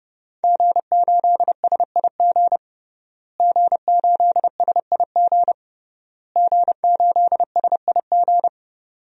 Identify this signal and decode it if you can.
Morse code, heard as a keyed tone, about 32 wpm, 700 Hz. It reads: G8SIG G8SIG G8SIG
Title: DXCC-100 Call Signs - 5 Characters - Repeated 3x - 30wpm